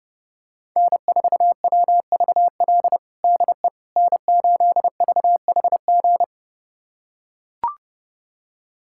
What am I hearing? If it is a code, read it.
Morse code, heard as a keyed tone, about 32 wpm, 700 Hz. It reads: N4WVL DE N8VHG
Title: USA Callsign DE Callsign - 5 Characters - 30wpm